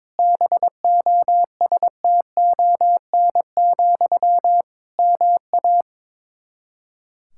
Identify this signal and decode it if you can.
Morse code, heard as a keyed tone, about 22 wpm, 700 Hz. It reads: BOSTON, MA
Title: K1USN SST QSO Practice 22wpm